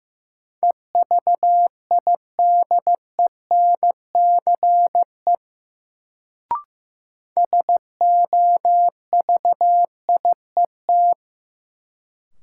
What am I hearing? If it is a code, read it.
Morse code, heard as a keyed tone, about 15 wpm, 700 Hz. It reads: EVIDENCE SOVIET
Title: Sets of 5 Words Encore - New Word Drill - 15wpm